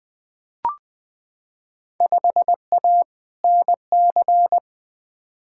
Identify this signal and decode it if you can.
Morse code, heard as a keyed tone, about 20 wpm, 700 Hz. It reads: 5A NC
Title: Field Day Exchanges - 20wpm